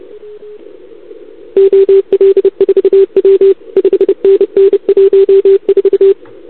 Morse code, heard as a keyed tone, about 30 wpm, 440 Hz. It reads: OL4W 5NN 14
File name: CQWWCW2021 / VA2UR